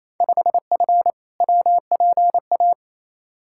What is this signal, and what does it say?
Morse code, heard as a keyed tone, about 28 wpm, 700 Hz. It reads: 5F WPA
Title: Field Day Exchanges Encore - 28wpm